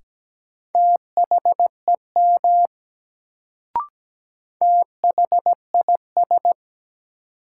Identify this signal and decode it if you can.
Morse code, heard as a keyed tone, about 17 wpm, 700 Hz. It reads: THEM THIS
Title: Sets of 2 Words - New Word Drill - 17wpm